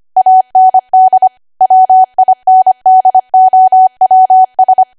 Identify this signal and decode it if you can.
Morse code, heard as a keyed tone, about 25 wpm, 750 Hz. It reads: AND WINDOWS
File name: Archive / 25 WPM / 25WPM